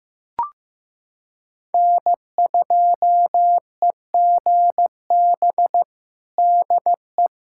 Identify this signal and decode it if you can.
Morse code, heard as a keyed tone, about 16 wpm, 700 Hz. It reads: N2EGB DE
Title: USA Callsign DE Callsign 15wpm